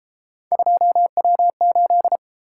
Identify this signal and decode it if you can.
Morse code, heard as a keyed tone, about 30 wpm, 700 Hz. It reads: 2W8